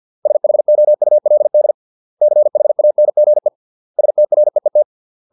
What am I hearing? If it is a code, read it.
Morse code, heard as a keyed tone, about 50 wpm, 600 Hz. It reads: SHOULD CHANGE STREET